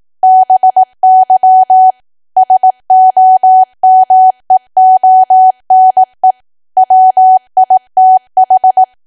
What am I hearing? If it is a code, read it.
Morse code, heard as a keyed tone, about 18 wpm, 750 Hz. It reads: BY SOMEONE WITH
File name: cwa / W1AW-Level2 / 18WPM